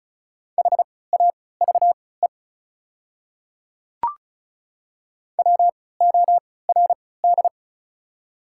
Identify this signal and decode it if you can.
Morse code, heard as a keyed tone, about 36 wpm, spaced as for 20 wpm, 690 Hz. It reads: HAVE WORD
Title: 3x Character Spacing - Top 100 Words - 35wpm